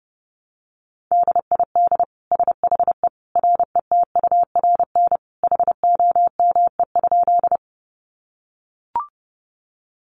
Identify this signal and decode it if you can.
Morse code, heard as a keyed tone, about 30 wpm, 700 Hz. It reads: DID SHE RETURN HOME?
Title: Sentences from Top 800 Words 30wpm